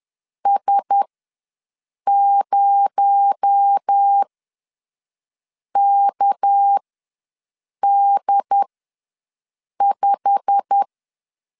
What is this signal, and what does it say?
Morse code, heard as a keyed tone, about 10 wpm, spaced as for 5 wpm, 800 Hz. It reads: S0KD5
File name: LEZIONI / 8 YJ45 / 8 YJ45 ws 10wpm 3843